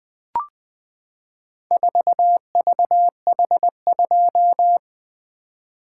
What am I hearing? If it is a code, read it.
Morse code, heard as a keyed tone, about 20 wpm, 700 Hz. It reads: 4VH2
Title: Four Letter-Number Combinations - 20wpm